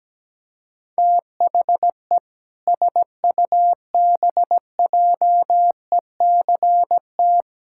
Morse code, heard as a keyed tone, about 17 wpm, 700 Hz. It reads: THE SUBJECT
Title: Sentences from Top 1600 Words 17wpm